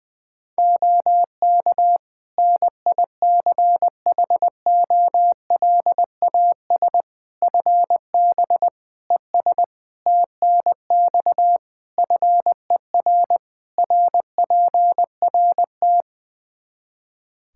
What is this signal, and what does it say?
Morse code, heard as a keyed tone, about 20 wpm, 700 Hz. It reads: OK NICHOLAS FB ES TNX FER RPRT
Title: Practice QSO - Exchange 2 - Info - 20wpm